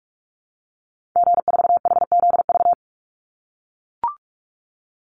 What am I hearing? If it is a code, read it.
Morse code, heard as a keyed tone, about 44 wpm, 700 Hz. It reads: G4HZV
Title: Call Signs - 45wpm